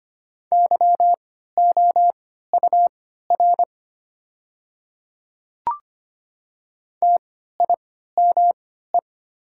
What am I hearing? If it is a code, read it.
Morse code, heard as a keyed tone, about 25 wpm, spaced as for 14 wpm, 690 Hz. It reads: YOUR TIME
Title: 3x Character Spacing - Top 300 Words - 25wpm